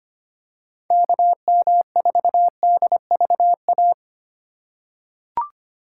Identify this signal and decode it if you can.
Morse code, heard as a keyed tone, about 25 wpm, 700 Hz. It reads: KM4DVA
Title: USA Call Signs - 6 Characters - 4x Character Spacing - 25wpm